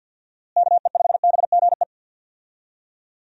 Morse code, heard as a keyed tone, about 46 wpm, 700 Hz. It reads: KE5BGE
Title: USA Call Signs - 6 Characters - Repeated 2x - 50wpm